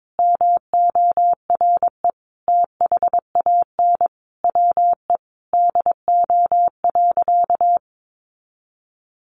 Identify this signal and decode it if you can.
Morse code, heard as a keyed tone, about 22 wpm, 700 Hz. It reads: MORE THAN WE DO.
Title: Sentences from Top 100 Words 22wpm